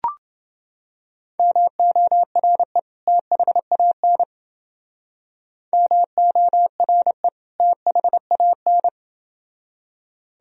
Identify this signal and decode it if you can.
Morse code, heard as a keyed tone, about 29 wpm, 700 Hz. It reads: MORE THAN MORE THAN